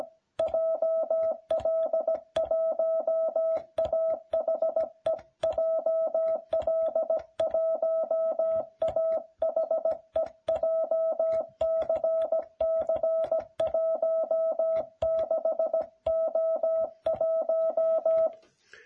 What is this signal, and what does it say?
Morse code, heard as a keyed tone, about 17 wpm, 660 Hz. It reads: JL1AHE JL1AHE JCC16O1